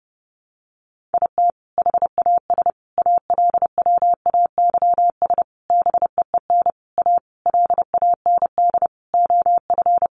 Morse code, heard as a keyed tone, about 30 wpm, 700 Hz. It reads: IT HAS ALWAYS BEEN A LAND OF
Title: Sentences from Top 1300 Words 30wpm